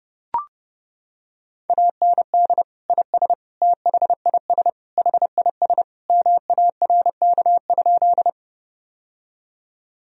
Morse code, heard as a keyed tone, about 30 wpm, 700 Hz. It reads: AND IS THIS HIS MARK?
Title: Sentences from Top 1000 Words - Review Part 1 - 30wpm